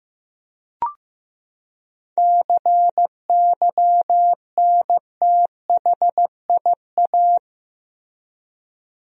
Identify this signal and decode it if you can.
Morse code, heard as a keyed tone, about 15 wpm, 700 Hz. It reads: CYNTHIA
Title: USA Common Names - 15wpm